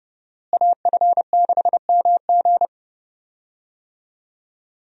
Morse code, heard as a keyed tone, about 30 wpm, 700 Hz. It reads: AF6MG
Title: USA Call Signs - 5 Characters - Repeated 3x - 30wpm